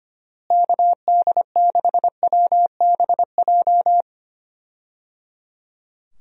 Morse code, heard as a keyed tone, about 25 wpm, 700 Hz. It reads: KD6WBJ
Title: USA Call Signs - 6 Characters - Repeated 3x - 25wpm